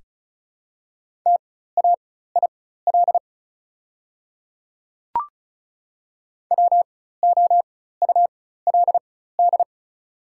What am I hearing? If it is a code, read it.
Morse code, heard as a keyed tone, about 36 wpm, spaced as for 16 wpm, 690 Hz. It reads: TAIL WOULD
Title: 4x Character Spacing - Top 500 Words - 35wpm